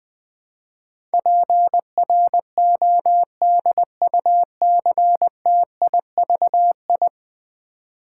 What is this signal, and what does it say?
Morse code, heard as a keyed tone, about 20 wpm, 700 Hz. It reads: PRODUCTIVI
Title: Build A Word - 5 Letters At A Time - 12-Letter Words - 20wpm